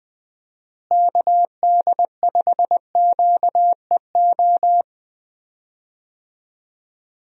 Morse code, heard as a keyed tone, about 20 wpm, 700 Hz. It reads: KD5QEO
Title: USA Call Signs - 6 Characters - Repeated 3x - 20wpm